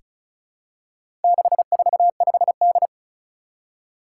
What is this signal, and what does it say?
Morse code, heard as a keyed tone, about 36 wpm, 700 Hz. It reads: 645D
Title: Four Challenging Letter-Number Combinations 35wpm